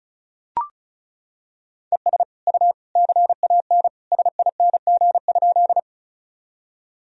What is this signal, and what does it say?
Morse code, heard as a keyed tone, about 34 wpm, 700 Hz. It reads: ES U CAN SING?
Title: Sentences from Top 400 Words and Abbreviations 35wpm